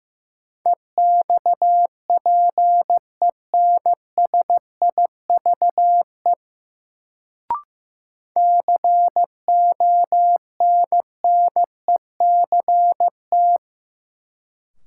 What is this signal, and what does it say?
Morse code, heard as a keyed tone, about 15 wpm, 700 Hz. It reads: EXPENSIVE CONNECT
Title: Sentences from Top 1500 Words - New Word Drill Alt - 15wpm